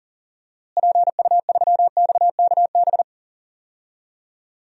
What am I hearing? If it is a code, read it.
Morse code, heard as a keyed tone, about 40 wpm, 700 Hz. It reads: PU3XKB
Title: DXCC-100 Call Signs - 6 Characters - Repeated 2x - 40wpm